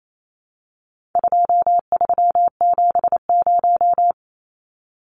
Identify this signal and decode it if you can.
Morse code, heard as a keyed tone, about 28 wpm, 700 Hz. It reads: 2370
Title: Numbers - 4-Digit 28wpm